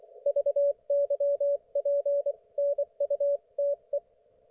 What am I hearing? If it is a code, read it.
Morse code, heard as a keyed tone, about 24 wpm, spaced as for 21 wpm, 570 Hz. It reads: VYPNUTE
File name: Archiv / CUC224 / ok1fhi cul224